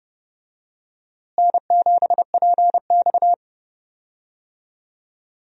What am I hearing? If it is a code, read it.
Morse code, heard as a keyed tone, about 30 wpm, 700 Hz. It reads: N7PX
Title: USA Call Signs - 4 Characters - Repeated 2x - 30wpm